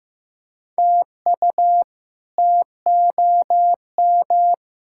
Morse code, heard as a keyed tone, about 15 wpm, 700 Hz. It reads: TU TOM
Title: K1USN SST QSO Practice - 15wpm